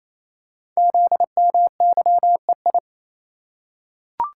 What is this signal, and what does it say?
Morse code, heard as a keyed tone, about 29 wpm, 700 Hz. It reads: ZMYEI
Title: Code Groups - 5 Letters - 28wpm